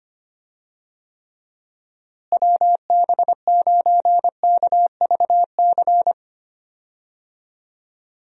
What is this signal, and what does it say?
Morse code, heard as a keyed tone, about 25 wpm, 700 Hz. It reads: WB9KVC
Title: USA Call Signs - 6 Characters - Repeated 3x - 25wpm